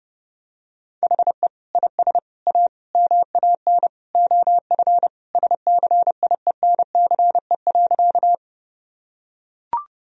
Morse code, heard as a keyed tone, about 30 wpm, 700 Hz. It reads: HE IS A MAN OF SCIENCE.
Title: Sentences from Top 1000 Words - Review Part 2 - 30wpm